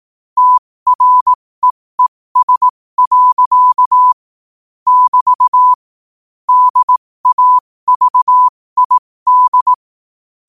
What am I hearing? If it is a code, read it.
Morse code, heard as a keyed tone, about 18 wpm, spaced as for 15 wpm, 1000 Hz. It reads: TREES. <BT> DAVID
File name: qotdfiles / 15wpm